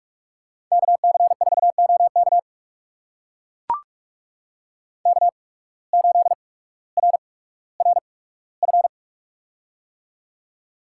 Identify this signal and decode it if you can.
Morse code, heard as a keyed tone, about 44 wpm, spaced as for 19 wpm, 700 Hz. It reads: KC4OK K8RRF
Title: USA Call Signs - 5 Characters - 8x Character Spacing - 45wpm